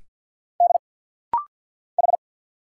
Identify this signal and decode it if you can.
Morse code, heard as a keyed tone, about 50 wpm, 690 Hz. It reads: D H
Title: Single Letters - Rapid-Fire 50wpm